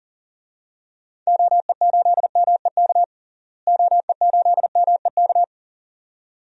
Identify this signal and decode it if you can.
Morse code, heard as a keyed tone, about 40 wpm, 700 Hz. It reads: OE8MEK OE8MEK
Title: DXCC-100 Call Signs - 6 Characters - Repeated 2x - 40wpm